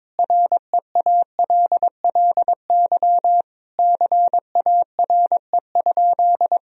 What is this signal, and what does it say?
Morse code, heard as a keyed tone, about 22 wpm, 700 Hz. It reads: REALLY CARE?
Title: Sentences from Top 1000 Words - Review Part 3 - 22wpm